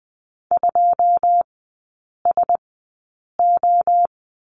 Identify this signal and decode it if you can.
Morse code, heard as a keyed tone, about 20 wpm, spaced as for 8 wpm, 700 Hz. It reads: 2SO
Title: 087 Lesson 15e - Number 5 - Three Characters 20wpm